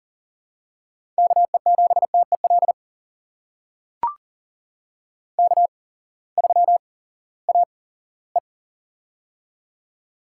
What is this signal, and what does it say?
Morse code, heard as a keyed tone, about 38 wpm, spaced as for 20 wpm, 700 Hz. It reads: KE7TEL K3AE
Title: USA Call Signs - 8x Character Spacing - 40wpm